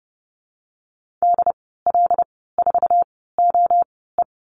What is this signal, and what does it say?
Morse code, heard as a keyed tone, about 30 wpm, spaced as for 17 wpm, 700 Hz. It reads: DL4OE
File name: Lesson 15g - Number 5 - Call Signs 30wpm